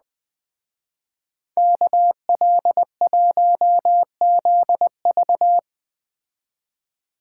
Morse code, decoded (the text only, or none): KL1ZV